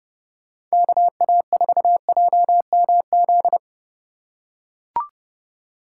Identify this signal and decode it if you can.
Morse code, heard as a keyed tone, about 30 wpm, 700 Hz. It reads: KA4JMZ
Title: USA Call Signs - 6 Characters - 2x Character Spacing - 30wpm